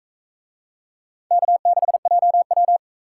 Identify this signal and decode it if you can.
Morse code, heard as a keyed tone, about 42 wpm, 700 Hz. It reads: K6JW